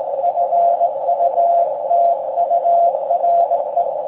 Morse code, heard as a keyed tone, about 18 wpm, 690 Hz, with noise. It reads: FUTURE
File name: forum / im / aq grimeton